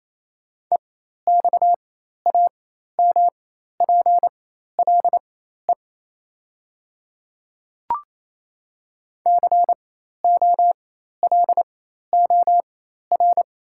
Morse code, heard as a keyed tone, about 27 wpm, spaced as for 13 wpm, 700 Hz. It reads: EXAMPLE COLOR
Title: 4x Character Spacing - Top 300 Words - 28wpm